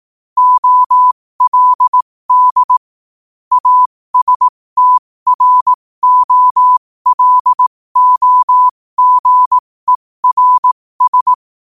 Morse code, heard as a keyed tone, about 18 wpm, spaced as for 15 wpm, 1000 Hz. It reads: OLD ASTROLOGERS
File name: qotdfiles / 15wpm